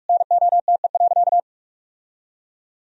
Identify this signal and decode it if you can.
Morse code, heard as a keyed tone, about 46 wpm, 700 Hz. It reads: NOTE.
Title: Sentences from Top 1000 Words - Review Part 4 - 45wpm